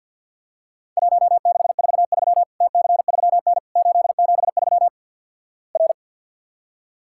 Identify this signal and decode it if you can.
Morse code, heard as a keyed tone, about 53 wpm, 700 Hz. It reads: 1643 TC3N 873 R
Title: Worked All Europe DX Contest - QTC Exchange - 50wpm